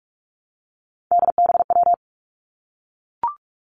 Morse code, heard as a keyed tone, about 46 wpm, 700 Hz. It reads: DBW
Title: Three Challenging Letter-Number Combinations 45wpm